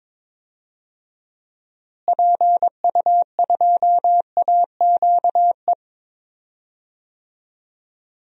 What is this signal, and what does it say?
Morse code, heard as a keyed tone, about 22 wpm, 700 Hz. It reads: PU2AQE
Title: DXCC-100 Call Signs - 6 Characters - Repeated 2x - 22wpm